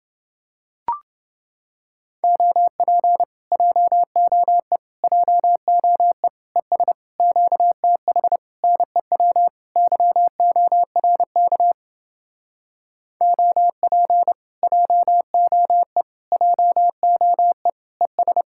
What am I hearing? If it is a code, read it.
Morse code, heard as a keyed tone, about 29 wpm, 700 Hz. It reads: OP JOE JOE ES QTH NEW YORK OP JOE JOE ES